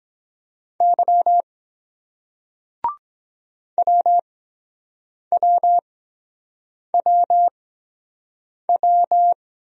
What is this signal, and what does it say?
Morse code, heard as a keyed tone, about 23 wpm, spaced as for 7 wpm, 700 Hz. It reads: Y WWWW